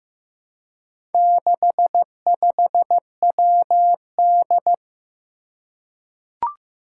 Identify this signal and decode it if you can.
Morse code, heard as a keyed tone, about 15 wpm, 700 Hz. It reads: 65WD
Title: Four Challenging Letter-Number Combinations 15wpm